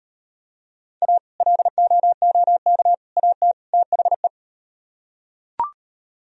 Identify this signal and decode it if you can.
Morse code, heard as a keyed tone, about 37 wpm, 700 Hz. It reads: A LOOK AT THE